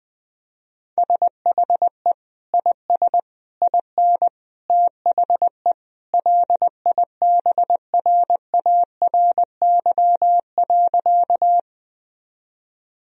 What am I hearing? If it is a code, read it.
Morse code, heard as a keyed tone, about 20 wpm, 700 Hz. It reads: SHE IS IN THE LIBRARY.
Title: Sentences from Top 1000 Words - Review Part 4 - 20wpm